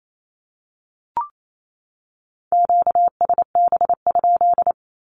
Morse code, heard as a keyed tone, about 28 wpm, 700 Hz. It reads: QSB?
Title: Q-Codes 28wpm